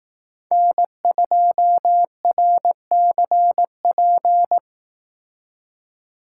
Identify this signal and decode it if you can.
Morse code, heard as a keyed tone, about 18 wpm, 700 Hz. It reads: N2RCP
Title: Speed Racing - USA 1x3 Call Signs 15wpm